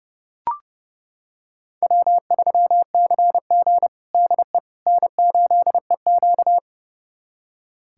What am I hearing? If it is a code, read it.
Morse code, heard as a keyed tone, about 30 wpm, 700 Hz. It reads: W3CG DE N8EQ
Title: USA Callsign DE Callsign - 4 Characters - 30wpm